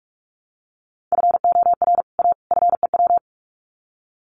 Morse code, heard as a keyed tone, about 44 wpm, 700 Hz. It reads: FOR A FEW
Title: Sets of 3 Words 45wpm